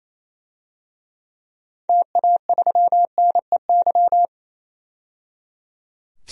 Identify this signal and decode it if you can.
Morse code, heard as a keyed tone, about 28 wpm, 700 Hz. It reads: TA3NEY
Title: DXCC-100 Call Signs - 6 Characters - 28wpm